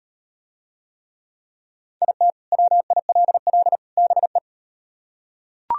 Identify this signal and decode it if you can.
Morse code, heard as a keyed tone, about 38 wpm, 700 Hz. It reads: IT WILL BE